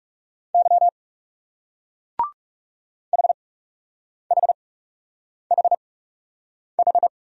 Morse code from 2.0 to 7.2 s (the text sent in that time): HHHH